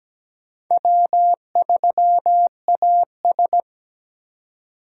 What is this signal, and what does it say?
Morse code, heard as a keyed tone, about 17 wpm, 700 Hz. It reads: W3AS
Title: USA Call Signs - 4 Characters - 3x Character Spacing - 17wpm